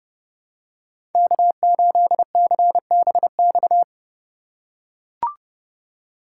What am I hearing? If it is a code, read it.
Morse code, heard as a keyed tone, about 30 wpm, 700 Hz. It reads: K8CBX
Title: USA Call Signs - 4x Character Spacing - 30wpm